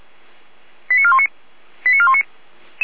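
Five-tone selective call (ZVEI), heard as quick runs of five tones, tones of about 70 ms. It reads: 89419 89419
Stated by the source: ZVEI five-tone call 85369, received on SDR.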